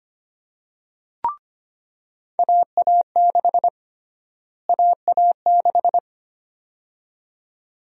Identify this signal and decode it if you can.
Morse code, heard as a keyed tone, about 25 wpm, 700 Hz. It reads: AA6 AA6
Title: USA Call Sign Prefixes - Repeated 2x - 25wpm